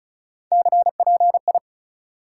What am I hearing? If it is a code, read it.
Morse code, heard as a keyed tone, about 36 wpm, 700 Hz. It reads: CPI
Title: Top 180 QSO Elements 35wpm